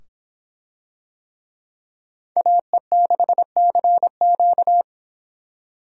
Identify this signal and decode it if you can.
Morse code, heard as a keyed tone, about 26 wpm, 700 Hz. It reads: AE6CQ